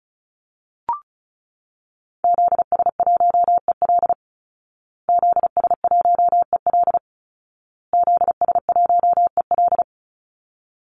Morse code, heard as a keyed tone, about 34 wpm, 700 Hz. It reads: ZS1EL ZS1EL ZS1EL
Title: Call Signs Repeated 2x - 35wpm